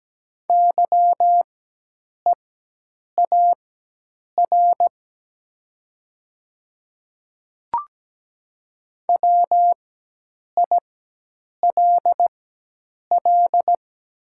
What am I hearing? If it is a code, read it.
Morse code, heard as a keyed tone, about 17 wpm, spaced as for 8 wpm, 700 Hz. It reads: YEAR WILL
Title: 4x Character Spacing - Top 500 Words - 17wpm